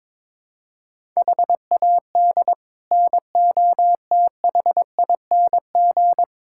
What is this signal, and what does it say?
Morse code, heard as a keyed tone, about 22 wpm, 700 Hz. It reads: HAD NOTHING
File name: practice / 22wpm / Build A Phrase - 5-Word Phrases - 22wpm